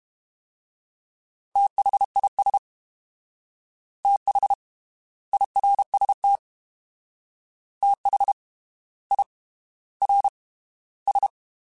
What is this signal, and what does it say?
Morse code, heard as a keyed tone, about 32 wpm, spaced as for 18 wpm, 800 Hz. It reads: THIS THIRST THIRS
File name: TH / TH mp3s / Fast TH @30